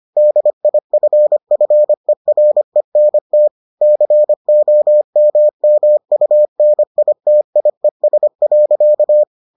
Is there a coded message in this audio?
Yes: Morse code from 0.2 to 9.2 s, heard as a keyed tone, about 25 wpm, 600 Hz. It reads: DIFFERENT COMMUNITIES.